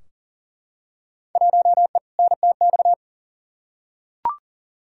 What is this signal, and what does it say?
Morse code, heard as a keyed tone, about 40 wpm, 700 Hz. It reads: 1E NTX
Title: Field Day Exchanges - 40wpm